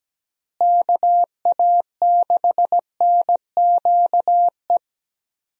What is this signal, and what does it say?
Morse code, heard as a keyed tone, about 17 wpm, 700 Hz. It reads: KA6NQE
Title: USA Call Signs - 6 Characters - Repeated 3x - 17wpm